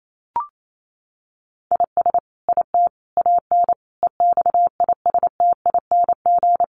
Morse code, heard as a keyed tone, about 28 wpm, 700 Hz. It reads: IS IT AN EXISTING
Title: Sentences from Top 1500 Words 28wpm